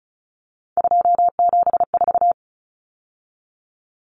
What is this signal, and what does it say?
Morse code, heard as a keyed tone, about 34 wpm, 700 Hz. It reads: 274
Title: Numbers - 3-Digit 35wpm